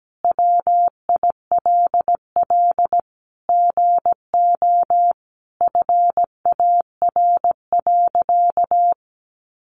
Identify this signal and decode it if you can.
Morse code, heard as a keyed tone, about 17 wpm, 700 Hz. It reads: WILL GO FAR.
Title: Sentences from Top 400 Words 17wpm